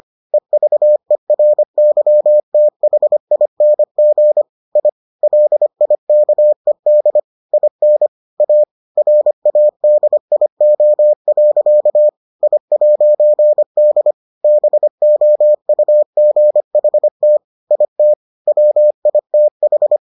Morse code, heard as a keyed tone, about 25 wpm, 600 Hz. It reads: EVERYTHING I LIKED IN A RADIO. I'D BOUGHT IT WITH